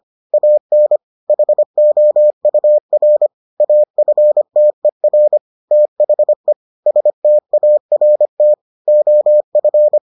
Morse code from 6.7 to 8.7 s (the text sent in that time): START